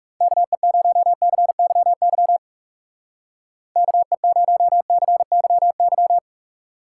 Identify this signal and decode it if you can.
Morse code, heard as a keyed tone, about 36 wpm, 700 Hz. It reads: KE0CYY KE0CYY
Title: Speed Racing - USA 2x3 Call Signs 30wpm